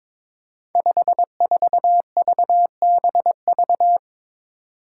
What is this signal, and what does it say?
Morse code, heard as a keyed tone, about 22 wpm, 700 Hz. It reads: 54VBV
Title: Five Challenging Letter-Number Combinations 22wpm